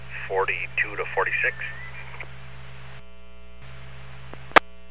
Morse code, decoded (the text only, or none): none